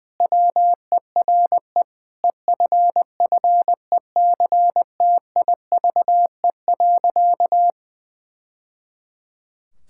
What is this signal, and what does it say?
Morse code, heard as a keyed tone, about 20 wpm, 700 Hz. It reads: WERE EFFECTIVE.
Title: Sentences from Top 1000 Words - Review Part 6 - 20wpm